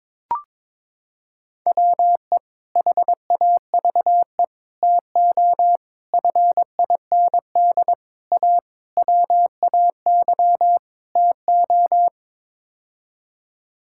Morse code, heard as a keyed tone, about 22 wpm, 700 Hz. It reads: WE HAVE TO FIND A WAY TO
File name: practice / 22wpm / Sets of 7 Words - 22wpm